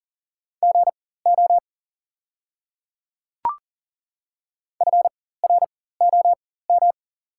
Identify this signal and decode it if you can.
Morse code, heard as a keyed tone, about 40 wpm, spaced as for 19 wpm, 700 Hz. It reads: GO FROM
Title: 4x Character Spacing - Top 200 Words - 40wpm